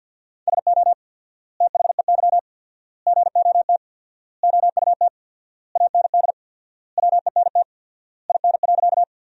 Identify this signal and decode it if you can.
Morse code, heard as a keyed tone, about 52 wpm, 700 Hz. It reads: SO THEY GOT OUT AND WENT IN.
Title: 4x Word-Spacing - Sentences from Top 300 Words - 50wpm